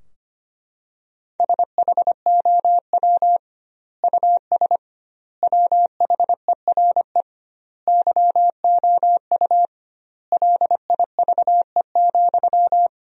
Morse code, heard as a keyed tone, about 25 wpm, 700 Hz. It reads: SHOW US WHERE YOU LIVE,
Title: 2x Word-Spacing - Sentences from Top 200 Words - 25wpm